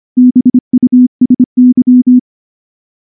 Morse code, heard as a keyed tone, about 25 wpm, 250 Hz. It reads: BUSY
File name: morse-training-25WPM